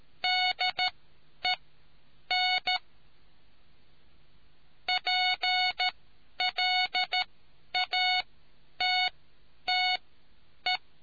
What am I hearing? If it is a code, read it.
Morse code, heard as a keyed tone, about 13 wpm, spaced as for 9 wpm, 740 Hz. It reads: DEN PLATTE